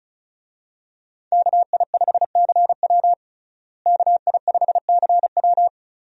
Morse code, heard as a keyed tone, about 34 wpm, 700 Hz. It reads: KI5CW KI5CW
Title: USA Call Signs - 5 Characters - Repeated 2x - 35wpm